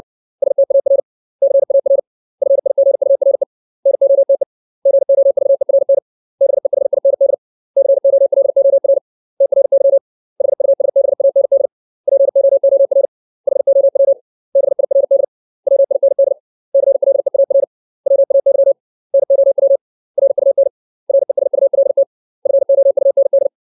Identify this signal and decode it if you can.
Morse code, heard as a keyed tone, about 60 wpm, 550 Hz. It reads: STAR CAR FIGURE NOTE GOVERN BEHIND COLOR TRY HUSBAND POOR HOW BIRD WIND CLEAR WAY NOW RUN RESULT FOUND
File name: 632 words / 632 Words 60WPM